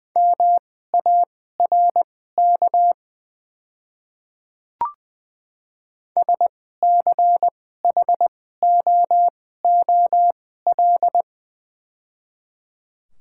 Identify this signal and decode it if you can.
Morse code, heard as a keyed tone, about 21 wpm, spaced as for 15 wpm, 690 Hz. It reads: MARK SCHOOL
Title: 2x Character Spacing - Top 300 Words - 20wpm